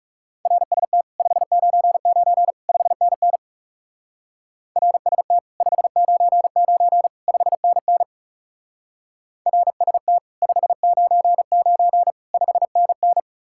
Morse code, heard as a keyed tone, about 37 wpm, 700 Hz. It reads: RST 599 5NN RST 599 5NN RST 599 5NN
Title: Speed-Racing Common QSO Phrases - 30wpm